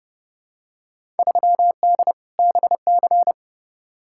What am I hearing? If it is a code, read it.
Morse code, heard as a keyed tone, about 29 wpm, 700 Hz. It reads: 3D BC
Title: Field Day Exchanges Encore - 30wpm